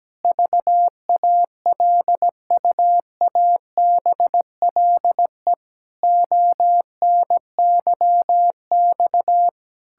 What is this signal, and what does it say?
Morse code, heard as a keyed tone, about 17 wpm, 700 Hz. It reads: VALUABLE ONYX
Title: Pangram Challenge v2 - 17wpm